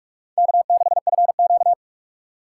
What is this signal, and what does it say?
Morse code, heard as a keyed tone, about 67 wpm, spaced as for 55 wpm, 700 Hz. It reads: K6FQ
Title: Speed Racing - USA 1x2 Call Signs 45wpm